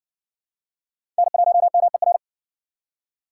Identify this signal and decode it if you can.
Morse code, heard as a keyed tone, about 53 wpm, 700 Hz. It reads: N1MER